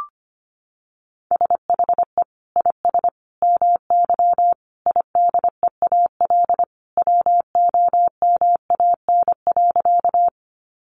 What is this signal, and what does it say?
Morse code, heard as a keyed tone, about 25 wpm, 700 Hz. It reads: SHE IS MY IDEAL WOMAN.
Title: Sentences from Top 1400 Words 25wpm